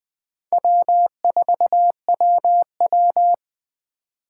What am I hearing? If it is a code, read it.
Morse code, heard as a keyed tone, about 21 wpm, 700 Hz. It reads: W4WW
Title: USA Call Signs - 4 Characters - 4x Character Spacing - 20wpm